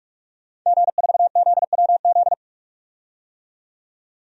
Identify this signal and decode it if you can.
Morse code, heard as a keyed tone, about 44 wpm, 690 Hz. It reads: G4ZWZ
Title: DXCC-100 Call Signs - 5 Characters - 45wpm